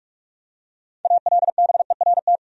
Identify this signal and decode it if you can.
Morse code, heard as a keyed tone, about 45 wpm, 700 Hz. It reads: ALBERT
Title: USA Common Names - 45wpm